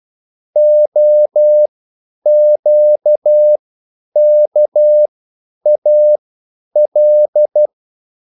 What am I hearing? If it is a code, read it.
Morse code, heard as a keyed tone, about 12 wpm, spaced as for 9 wpm, 600 Hz. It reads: OQKAL